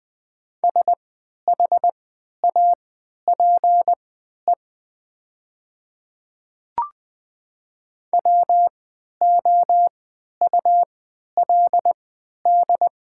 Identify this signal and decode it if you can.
Morse code, heard as a keyed tone, about 20 wpm, spaced as for 11 wpm, 700 Hz. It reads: SHAPE WOULD
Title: 3x Character Spacing - Top 500 Words - 20wpm